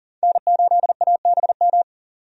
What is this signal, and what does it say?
Morse code, heard as a keyed tone, about 40 wpm, 700 Hz. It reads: N8ABM
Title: USA Call Signs - 5 Characters - 40wpm